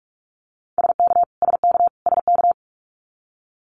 Morse code, heard as a keyed tone, about 46 wpm, 700 Hz. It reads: SK SK SK
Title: Canadian Provinces and Territories - Repeated 3x - 45wpm